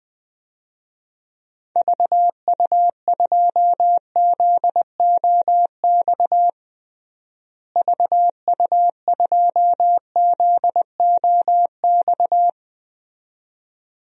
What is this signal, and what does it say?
Morse code, heard as a keyed tone, about 20 wpm, 700 Hz. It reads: VU2ZOX VU2ZOX
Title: DXCC-100 Call Signs - 6 Characters - Repeated 2x - 20wpm